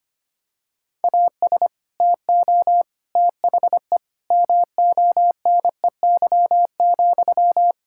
Morse code, heard as a keyed tone, about 25 wpm, 700 Hz. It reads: AS TO THE MONEY,
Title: Sentences from Top 1000 Words - Review Part 5 - 25wpm